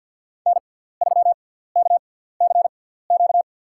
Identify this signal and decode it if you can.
Morse code, heard as a keyed tone, about 51 wpm, spaced as for 17 wpm, 700 Hz. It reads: N3KCQ
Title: USA Call Signs - 5 Characters - 6x Character Spacing - 50wpm